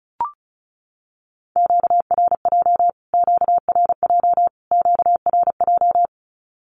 Morse code, heard as a keyed tone, about 34 wpm, 700 Hz. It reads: QRJ QRJ QRJ
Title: Q-Codes Repeated 3x 35wpm